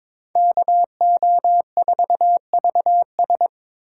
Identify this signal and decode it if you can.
Morse code, heard as a keyed tone, about 26 wpm, 700 Hz. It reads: KO4VS
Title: Speed Racing - USA Call Signs 22wpm